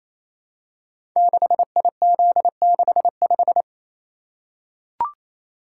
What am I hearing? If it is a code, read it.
Morse code, heard as a keyed tone, about 27 wpm, 700 Hz. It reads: 6IZ65
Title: Code Groups - 5 Letter:Numbers - 28wpm